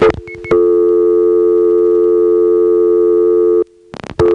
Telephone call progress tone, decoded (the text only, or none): dial tone